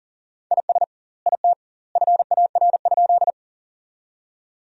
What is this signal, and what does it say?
Morse code, heard as a keyed tone, about 40 wpm, 700 Hz. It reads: IS IT FAR?
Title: 2x Word-Spacing - Sentences from Top 300 Words - 40wpm